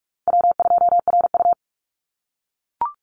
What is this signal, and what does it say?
Morse code, heard as a keyed tone, about 44 wpm, 700 Hz. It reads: W2RU
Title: Easy Call Signs - 45wpm